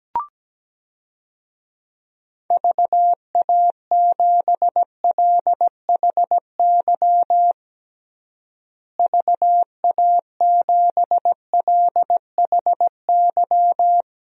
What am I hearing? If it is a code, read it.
Morse code, heard as a keyed tone, about 17 wpm, 700 Hz. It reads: VA7LHY VA7LHY
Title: DXCC-100 Call Signs - 6 Characters - Repeated 3x - 17wpm